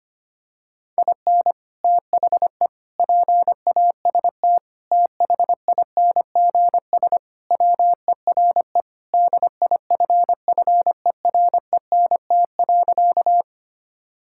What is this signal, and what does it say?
Morse code, heard as a keyed tone, about 25 wpm, 700 Hz. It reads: IN THE PAST THINGS WERE DIFFERENT.
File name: practice / 25wpm / Sentences with 3-Word Phrases - 25wpm